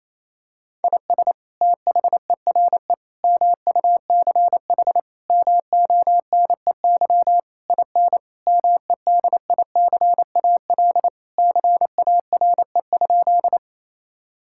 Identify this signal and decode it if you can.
Morse code, heard as a keyed tone, about 28 wpm, 700 Hz. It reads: IS THERE MUCH MONEY IN MEDICAL CARE?
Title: Sentences from Top 1000 Words - Review Part 5 - 28wpm